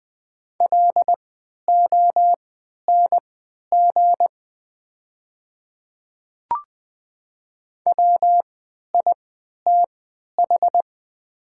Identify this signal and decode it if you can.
Morse code, heard as a keyed tone, about 20 wpm, spaced as for 11 wpm, 700 Hz. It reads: LONG WITH
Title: 3x Character Spacing - Top 100 Words - 20wpm